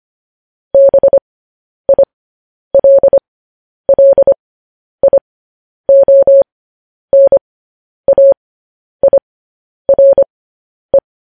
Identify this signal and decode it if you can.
Morse code, heard as a keyed tone, about 25 wpm, spaced as for 10 wpm, 550 Hz. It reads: BILLIONAIRE
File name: podcast / News headlines-1x-25x10